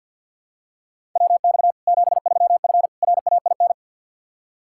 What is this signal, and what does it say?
Morse code, heard as a keyed tone, about 51 wpm, 700 Hz. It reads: WX 73F RAIN
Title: Practice QSO - Exchange 2 - Info - 50wpm